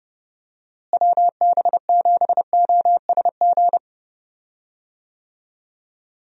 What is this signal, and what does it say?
Morse code, heard as a keyed tone, about 30 wpm, 700 Hz. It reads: WB7OSG
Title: USA Call Signs - 6 Characters - Repeated 3x - 30wpm